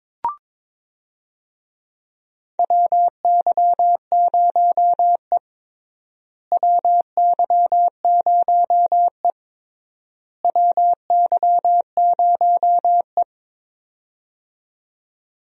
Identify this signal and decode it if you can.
Morse code, heard as a keyed tone, about 22 wpm, 700 Hz. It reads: WY0E WY0E WY0E
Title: USA Call Signs - 4 Characters - Repeated 3x - 22wpm